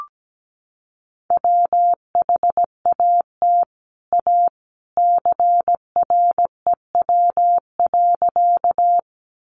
Morse code, heard as a keyed tone, about 17 wpm, 700 Hz. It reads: WHAT A CREW.
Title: Sentences from Top 1500 Words 17wpm